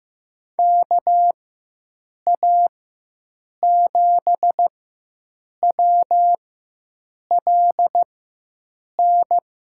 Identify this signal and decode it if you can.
Morse code, heard as a keyed tone, about 15 wpm, spaced as for 7 wpm, 700 Hz. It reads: KA7WLN
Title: USA Call Signs - 6 Characters - 4x Character Spacing - 15wpm